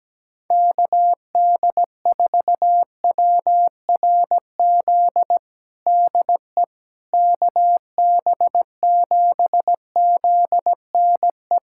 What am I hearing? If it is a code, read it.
Morse code, heard as a keyed tone, about 17 wpm, 700 Hz. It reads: KD4WRZ DE KB7ZNE
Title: USA Callsign DE Callsign - 6 Characters - 17wpm